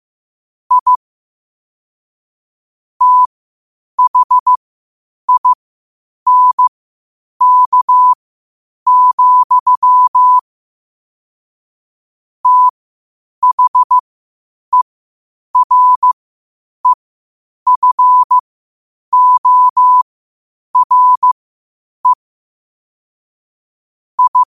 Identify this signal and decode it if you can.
Morse code, heard as a keyed tone, about 15 wpm, spaced as for 8 wpm, 1000 Hz. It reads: I THINK, THEREFORE I
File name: qotdfiles / 8wpm